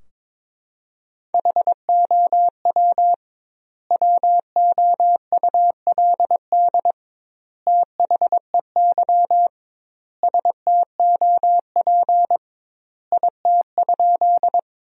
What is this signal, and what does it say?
Morse code, heard as a keyed tone, about 21 wpm, 700 Hz. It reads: HOW WOULD THEY STOP IT?
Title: 2x Word-Spacing - Sentences from Top 300 Words - 22wpm